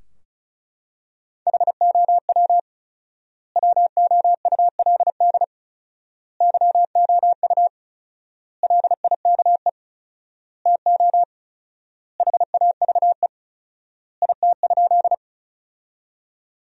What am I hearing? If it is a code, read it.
Morse code, heard as a keyed tone, about 36 wpm, 700 Hz. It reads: HOW WOULD YOU LIKE TO HAVE IT?
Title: 4x Word-Spacing - Sentences from Top 100 Words - 35wpm